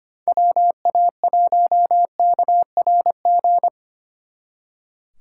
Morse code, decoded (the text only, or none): WA1KRG